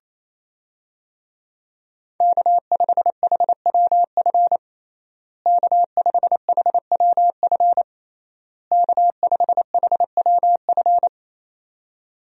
Morse code, heard as a keyed tone, about 28 wpm, 700 Hz. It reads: K5HWF K5HWF K5HWF
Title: USA Call Signs - 5 Characters - Repeated 3x - 28wpm